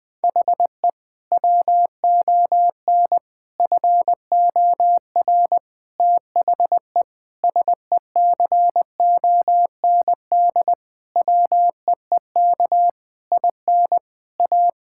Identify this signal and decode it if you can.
Morse code, heard as a keyed tone, about 21 wpm, 700 Hz. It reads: HE WON FOR THE SECOND WEEK IN A
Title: Sentences with 7-Word Phrases - 20wpm